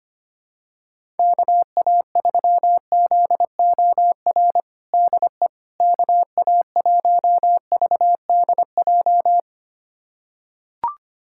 Morse code, heard as a keyed tone, about 25 wpm, 700 Hz. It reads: KA3ZOR DE KA1VDJ
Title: USA Callsign DE Callsign - 6 Characters - 25wpm